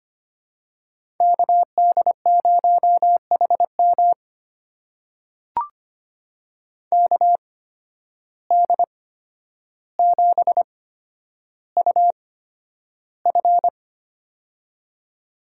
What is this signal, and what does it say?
Morse code, heard as a keyed tone, about 25 wpm, spaced as for 11 wpm, 700 Hz. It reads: KD0HM KD7UF
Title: USA Call Signs - 5 Characters - 8x Character Spacing - 25wpm